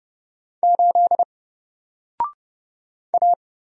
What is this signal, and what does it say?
Morse code, heard as a keyed tone, about 29 wpm, 700 Hz. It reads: 8 A
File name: Single-Letter-Number-30wpm